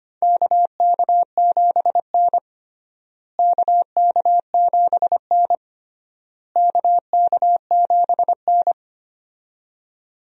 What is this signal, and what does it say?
Morse code, heard as a keyed tone, about 25 wpm, 700 Hz. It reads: KK7N KK7N KK7N
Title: USA Call Signs - 4 Characters - Repeated 3x - 25wpm